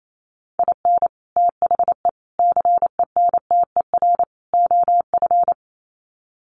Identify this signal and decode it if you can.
Morse code, heard as a keyed tone, about 28 wpm, 700 Hz. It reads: IN THE CENTER OF
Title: Sets of 4 Words 28wpm